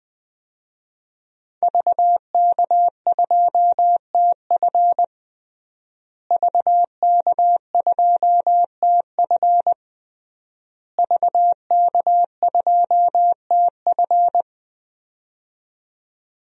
Morse code, heard as a keyed tone, about 20 wpm, 700 Hz. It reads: VK2TF VK2TF VK2TF
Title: DXCC-100 Call Signs - 5 Characters - Repeated 3x - 20wpm